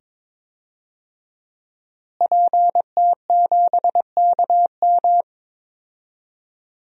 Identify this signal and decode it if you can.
Morse code, heard as a keyed tone, about 22 wpm, 700 Hz. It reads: PT7KM
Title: DXCC-100 Call Signs - 5 Characters - Repeated 2x - 22wpm